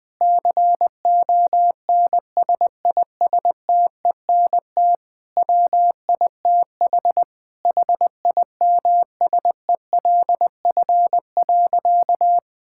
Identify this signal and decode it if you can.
Morse code, heard as a keyed tone, about 20 wpm, 700 Hz. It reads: CONSISTENT WITH HIMSELF.
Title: Sentences from Top 1600 Words 20wpm